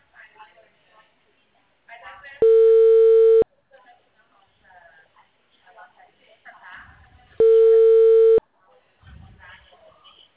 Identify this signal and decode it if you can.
Ringback tone (China, 450 Hz), with speech.
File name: out- -2006- - - .18